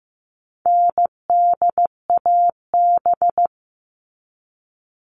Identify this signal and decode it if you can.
Morse code, heard as a keyed tone, about 15 wpm, 700 Hz. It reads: NDAB
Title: All Cut Numbers - 4-Digit 15wpm